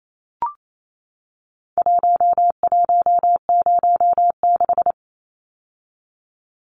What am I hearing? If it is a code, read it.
Morse code, heard as a keyed tone, about 28 wpm, 700 Hz. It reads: 1106
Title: Numbers - Contest Style - 4-Digit 28wpm